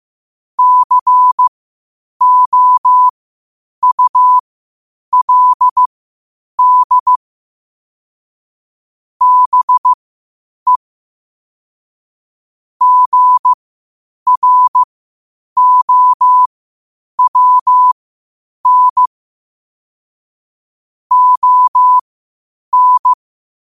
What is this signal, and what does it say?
Morse code, heard as a keyed tone, about 15 wpm, spaced as for 8 wpm, 1000 Hz. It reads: COULD BE GROWN ON
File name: qotdfiles / 8wpm